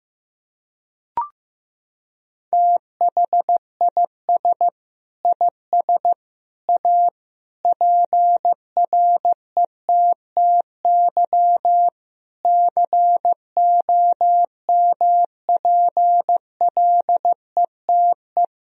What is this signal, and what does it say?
Morse code, heard as a keyed tone, about 15 wpm, 700 Hz. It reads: THIS IS A PRETTY COMPLETE
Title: Sentences from Top 1000 Words - Review Part 2 - 15wpm